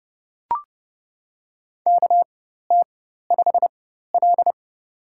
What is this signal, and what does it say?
Morse code, heard as a keyed tone, about 30 wpm, spaced as for 14 wpm, 700 Hz. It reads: KT5L
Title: USA Call Signs - 4 Characters - 4x Character Spacing - 30wpm